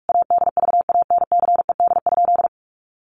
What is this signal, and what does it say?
Morse code, heard as a keyed tone, about 44 wpm, 700 Hz. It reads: ADVANCED?
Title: Sentences from Top 1200 Words 45wpm